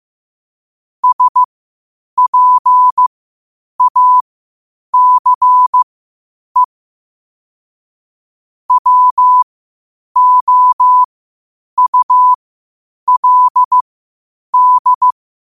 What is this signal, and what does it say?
Morse code, heard as a keyed tone, about 15 wpm, spaced as for 8 wpm, 1000 Hz. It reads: SPACE WOULD
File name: qotdfiles / 8wpm